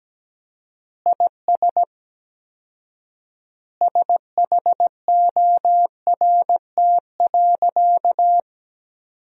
Morse code, heard as a keyed tone, about 17 wpm, 700 Hz. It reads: IS SHORT.
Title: 4x Word-Spacing - Sentences from Top 500 Words - 17wpm